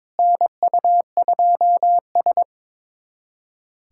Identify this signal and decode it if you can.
Morse code, heard as a keyed tone, about 22 wpm, 700 Hz. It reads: NU2S
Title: USA Callsign DE Callsign - 4 Characters - 22wpm